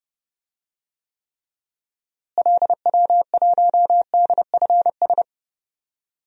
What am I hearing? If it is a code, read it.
Morse code, heard as a keyed tone, about 30 wpm, 700 Hz. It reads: LW1DFS
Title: DXCC-100 Call Signs - 6 Characters - Repeated 2x - 30wpm